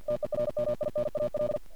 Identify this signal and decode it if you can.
Morse code, heard as a keyed tone, about 51 wpm, 600 Hz. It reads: TERMINAL
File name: archived / CRSnet